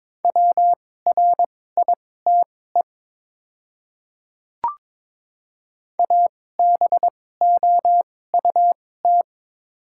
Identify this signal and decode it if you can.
Morse code, heard as a keyed tone, about 22 wpm, spaced as for 16 wpm, 700 Hz. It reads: WRITE ABOUT
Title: 2x Character Spacing - Top 300 Words - 22wpm